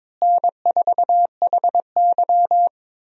Morse code, heard as a keyed tone, about 22 wpm, 700 Hz. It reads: N4HY
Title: USA Call Signs - 4 Characters - Repeated 2x - 22wpm